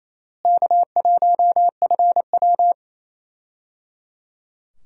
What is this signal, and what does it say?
Morse code, heard as a keyed tone, about 28 wpm, 700 Hz. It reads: K1FW
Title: USA Call Signs - 4 Characters - Repeated 3x - 28wpm